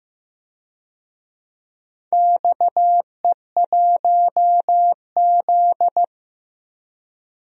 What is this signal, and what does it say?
Morse code, heard as a keyed tone, about 15 wpm, 700 Hz. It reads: XE1Z
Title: DXCC-100 Call Signs - 4 Characters - 15wpm